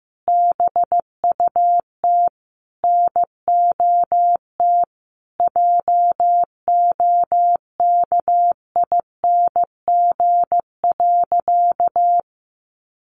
Morse code, heard as a keyed tone, about 15 wpm, 700 Hz. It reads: BUT NOT JOKING.
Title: A Sound Of Thunder 15wpm